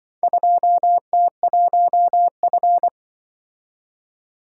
Morse code, heard as a keyed tone, about 25 wpm, 700 Hz. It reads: 2T1F
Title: Speed-Racing - Four Letter-Number Combinations - 20wpm